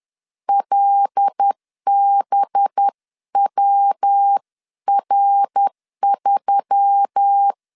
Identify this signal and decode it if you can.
Morse code, heard as a keyed tone, about 10 wpm, 800 Hz. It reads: LBWR3
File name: LEZIONI / 6 PKX0 / 6 PK 10wpm 2436